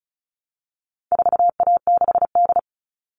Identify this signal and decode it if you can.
Morse code, heard as a keyed tone, about 34 wpm, 700 Hz. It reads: 4A6D
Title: All Cut Numbers - 4-Digit 35wpm